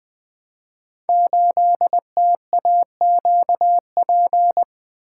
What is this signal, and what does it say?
Morse code, heard as a keyed tone, about 20 wpm, 700 Hz. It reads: 8TAQP
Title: Code Groups - 5 Letter:Numbers - 20wpm